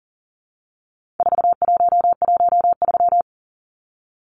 Morse code, heard as a keyed tone, about 40 wpm, 700 Hz. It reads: 4113
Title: Numbers - 4-Digit 40wpm